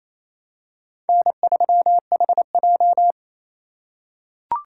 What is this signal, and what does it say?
Morse code, heard as a keyed tone, about 28 wpm, 700 Hz. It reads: N3HJ
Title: USA Call Signs - 4 Characters - 4x Character Spacing - 28wpm